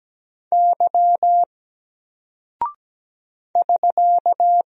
Morse code, heard as a keyed tone, about 17 wpm, 700 Hz. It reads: Y <SK>
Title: Single Character (Letter:Number:Punctuation:Prosign) Alt 17wpm